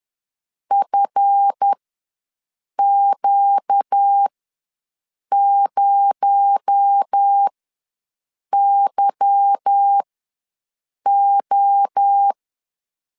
Morse code, heard as a keyed tone, about 10 wpm, spaced as for 6 wpm, 800 Hz. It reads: FQ0YO
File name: LEZIONI / 8 YJ45 / 8 YJ ws 10wpm 4043